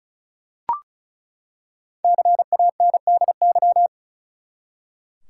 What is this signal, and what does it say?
Morse code, heard as a keyed tone, about 36 wpm, 700 Hz. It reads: CANDY
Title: Five Letter Words - 35wpm